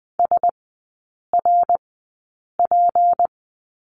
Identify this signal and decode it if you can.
Morse code, heard as a keyed tone, about 20 wpm, spaced as for 8 wpm, 700 Hz. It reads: SRP
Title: 152 Lesson 25e - Letter p - Three Characters 20wpm